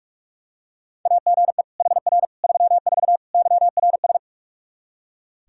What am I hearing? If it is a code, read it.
Morse code, heard as a keyed tone, about 46 wpm, 700 Hz. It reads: AGE HR 34 YRS
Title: Practice QSO - Exchange 3 - More Info - 45wpm